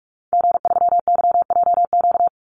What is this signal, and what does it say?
Morse code, heard as a keyed tone, about 44 wpm, 700 Hz. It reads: G3YJQ
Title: Call Signs - 45wpm